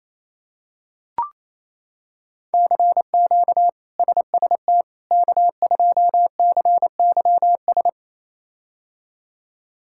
Morse code, heard as a keyed tone, about 28 wpm, 700 Hz. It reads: CQ SST K2CYS
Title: K1USN SST QSO Practice - 28wpm